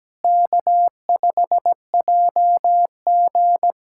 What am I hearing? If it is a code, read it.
Morse code, heard as a keyed tone, about 17 wpm, 700 Hz. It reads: K5JG
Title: USA Callsign DE Callsign - 4 Characters - 17wpm